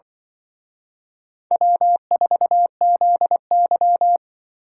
Morse code, heard as a keyed tone, about 24 wpm, 700 Hz. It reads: W4ZY
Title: Speed Racing - USA Call Signs 20wpm